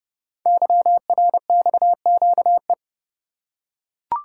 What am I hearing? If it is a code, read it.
Morse code, heard as a keyed tone, about 30 wpm, 700 Hz. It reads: YRXQE
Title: Code Groups - 5 Letters - 30wpm